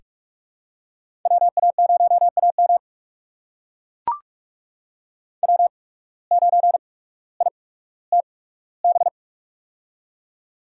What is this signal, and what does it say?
Morse code, heard as a keyed tone, about 44 wpm, spaced as for 19 wpm, 700 Hz. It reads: WA0AM W9ITB
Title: USA Call Signs - 5 Characters - 8x Character Spacing - 45wpm